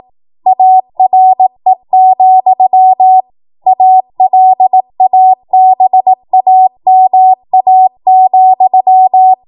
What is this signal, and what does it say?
Morse code, heard as a keyed tone, about 18 wpm, 750 Hz. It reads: ARE, ALABAMA,
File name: Morse / 18WPM